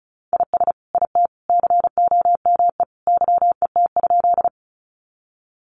Morse code, heard as a keyed tone, about 34 wpm, 700 Hz. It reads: IS IT COME YET?
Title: Sentences from Top 500 Words 35wpm